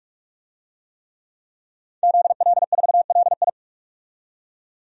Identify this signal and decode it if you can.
Morse code, heard as a keyed tone, about 48 wpm, 700 Hz. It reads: ZL4LI
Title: DXCC-100 Call Signs - 5 Characters - 45wpm